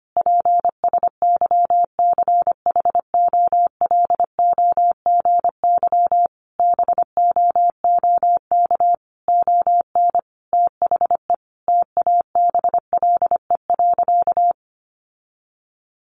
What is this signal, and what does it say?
Morse code, heard as a keyed tone, about 25 wpm, 700 Hz. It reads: PSYCHOLOGY BOOK ON THE TABLE.
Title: Sentences from Top 1400 Words 25wpm